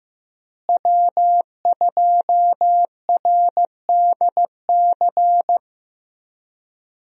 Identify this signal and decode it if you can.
Morse code, heard as a keyed tone, about 15 wpm, 700 Hz. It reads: W2RDC
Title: USA Call Signs - 5 Characters - Repeated 3x - 15wpm